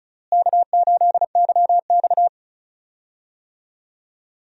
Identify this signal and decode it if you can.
Morse code, heard as a keyed tone, about 36 wpm, 700 Hz. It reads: K8YX
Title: USA Call Signs - 4 Characters - Repeated 2x - 35wpm